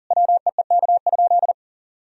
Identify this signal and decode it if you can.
Morse code, heard as a keyed tone, about 37 wpm, 700 Hz. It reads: WEEK?
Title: Sentences from Top 1000 Words - Review Part 2 - 40wpm